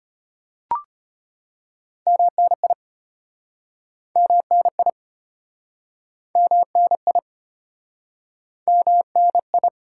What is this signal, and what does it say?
Morse code, heard as a keyed tone, about 35 wpm, 700 Hz. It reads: MNI MNI MNI MNI